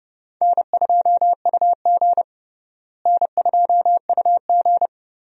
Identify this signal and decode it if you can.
Morse code, heard as a keyed tone, about 32 wpm, 700 Hz. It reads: N2UG N2UG
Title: USA Call Signs - 4 Characters - Repeated 3x - 30wpm